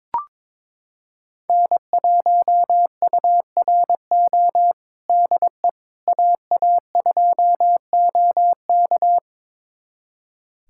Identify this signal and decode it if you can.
Morse code, heard as a keyed tone, about 22 wpm, 700 Hz. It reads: N1URO DE AA2OK
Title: USA Callsign DE Callsign - 5 Characters - 22wpm